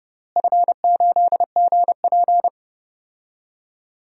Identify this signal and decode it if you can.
Morse code, heard as a keyed tone, about 30 wpm, 700 Hz. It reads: F8GP
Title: DXCC-100 Call Signs - 4 Characters - Repeated 2x - 30wpm